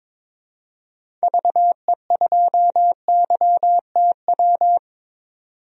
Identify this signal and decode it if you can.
Morse code, heard as a keyed tone, about 22 wpm, 700 Hz. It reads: VE2YTW
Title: DXCC-100 Call Signs - 6 Characters - 22wpm